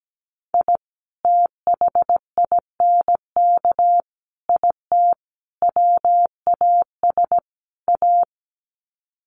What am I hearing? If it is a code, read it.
Morse code, heard as a keyed tone, about 17 wpm, 700 Hz. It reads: I THINK IT WAS A
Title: Sets of 5 Words 17wpm